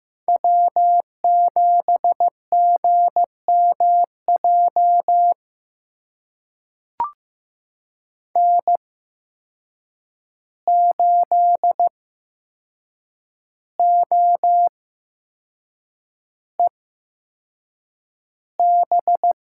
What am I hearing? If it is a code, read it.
Morse code, heard as a keyed tone, about 15 wpm, spaced as for 6 wpm, 700 Hz. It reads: W7GMJ N8OEB
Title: USA Call Signs - 5 Characters - 8x Character Spacing - 15wpm